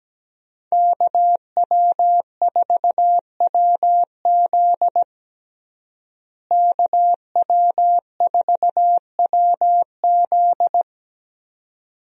Morse code, heard as a keyed tone, about 17 wpm, 700 Hz. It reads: KW4WZ KW4WZ
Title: USA Call Signs - 5 Characters - Repeated 2x - 17wpm